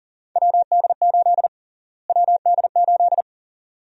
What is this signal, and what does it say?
Morse code, heard as a keyed tone, about 40 wpm, 700 Hz. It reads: WD8 WD8
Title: USA Call Sign Prefixes - Repeated 2x - 40wpm